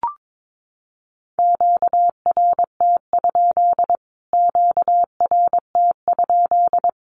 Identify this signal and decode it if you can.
Morse code, heard as a keyed tone, about 22 wpm, 700 Hz. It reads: QRT? QRT?
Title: Q-Codes Repeated 3x 22wpm